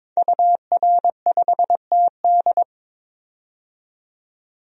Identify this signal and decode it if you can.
Morse code, heard as a keyed tone, about 22 wpm, 700 Hz. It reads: UR5TD
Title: DXCC-100 Call Signs - 5 Characters - 22wpm